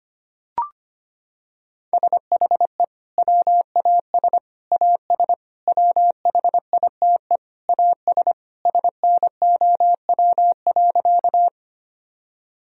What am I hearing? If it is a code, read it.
Morse code, heard as a keyed tone, about 25 wpm, 700 Hz. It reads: SHE WAS AS WHITE AS SNOW.
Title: Sentences from Top 1000 Words - Review Part 3 - 25wpm